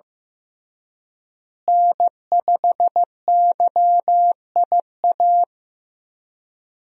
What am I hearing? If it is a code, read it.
Morse code, heard as a keyed tone, about 15 wpm, 700 Hz. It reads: N5YIA
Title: USA Call Signs - 5 Characters - Repeated 3x - 15wpm